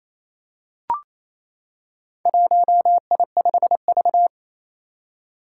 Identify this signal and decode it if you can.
Morse code, heard as a keyed tone, about 27 wpm, 700 Hz. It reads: 1I5V
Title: Four Challenging Letter-Number Combinations 28wpm